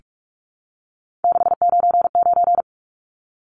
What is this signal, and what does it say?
Morse code, heard as a keyed tone, about 44 wpm, 700 Hz. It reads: 699
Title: Numbers - 3-Digit 45wpm